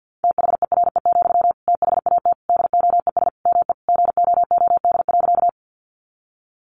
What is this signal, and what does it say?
Morse code, heard as a keyed tone, about 50 wpm, 700 Hz. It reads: THERE, THAT DOES ME GOOD.
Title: Sentences from Top 200 Words 50wpm